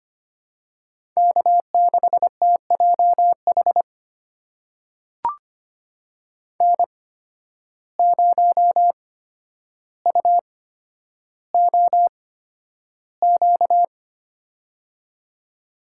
Morse code, heard as a keyed tone, about 25 wpm, spaced as for 11 wpm, 700 Hz. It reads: K6TJH N0UOQ
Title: USA Call Signs - 5 Characters - 8x Character Spacing - 25wpm